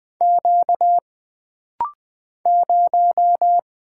Morse code, heard as a keyed tone, about 20 wpm, 700 Hz. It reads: Q 0
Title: Single Character (Letter:Number:Punctuation:Prosign) - Rapid-Fire - 20wpm